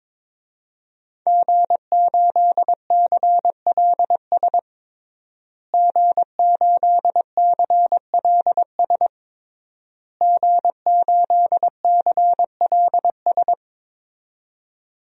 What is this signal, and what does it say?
Morse code, heard as a keyed tone, about 22 wpm, 700 Hz. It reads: G8CLS G8CLS G8CLS
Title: DXCC-100 Call Signs - 5 Characters - Repeated 3x - 22wpm